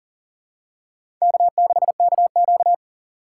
Morse code, heard as a keyed tone, about 40 wpm, 690 Hz. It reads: K6KQ
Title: USA Call Signs - 4 Characters - 6x Character Spacing - 40wpm